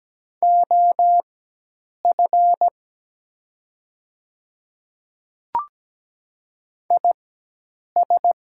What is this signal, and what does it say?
Morse code, heard as a keyed tone, about 17 wpm, spaced as for 8 wpm, 700 Hz. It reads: OF IS
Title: 4x Character Spacing - Top 200 Words - 17wpm